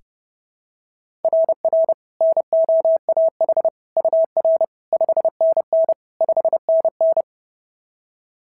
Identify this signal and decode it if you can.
Morse code, heard as a keyed tone, about 30 wpm, 690 Hz. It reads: RR NOAH UR 5NN 5NN
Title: SOTA QSO Practice 30wpm